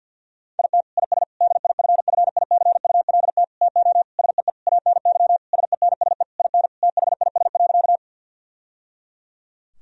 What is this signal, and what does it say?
Morse code, heard as a keyed tone, about 50 wpm, 700 Hz. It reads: IT IS DIFFICULT TO SEE ANY SENSE IN THIS.
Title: Sentences from Top 1000 Words - Review Part 4 - 50wpm